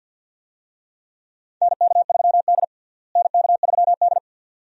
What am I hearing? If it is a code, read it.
Morse code, heard as a keyed tone, about 48 wpm, 690 Hz. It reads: NK3D NK3D
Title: USA Call Signs - 4 Characters - Repeated 2x - 50wpm